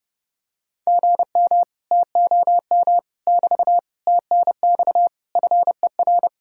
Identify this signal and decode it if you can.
Morse code, heard as a keyed tone, about 31 wpm, 700 Hz. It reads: GM TOM <BT> TNX FER